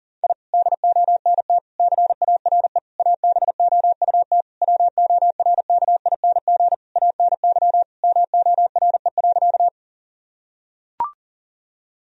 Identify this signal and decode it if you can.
Morse code, heard as a keyed tone, about 40 wpm, 700 Hz. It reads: I DONT CARE ABOUT WORKING ANY MORE.
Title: Sentences from Top 1000 Words - Review Part 4 - 40wpm